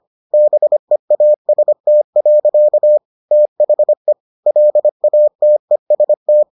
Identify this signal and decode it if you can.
Morse code, heard as a keyed tone, about 25 wpm, 600 Hz. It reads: BEAST. THE LATEST